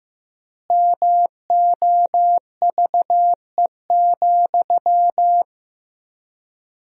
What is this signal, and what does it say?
Morse code, heard as a keyed tone, about 16 wpm, 700 Hz. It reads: MOVE,
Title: 4x Word-Spacing - Sentences from Top 300 Words - 15wpm